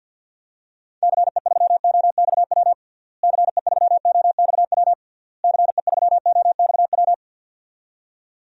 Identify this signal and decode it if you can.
Morse code, heard as a keyed tone, about 47 wpm, 700 Hz. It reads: CE3OXW CE3OXW CE3OXW
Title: DXCC-100 Call Signs - 6 Characters - Repeated 3x - 50wpm